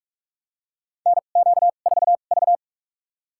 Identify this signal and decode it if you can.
Morse code, heard as a keyed tone, about 46 wpm, spaced as for 33 wpm, 700 Hz. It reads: NQ4V
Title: USA Call Signs - 2x Character Spacing - 45wpm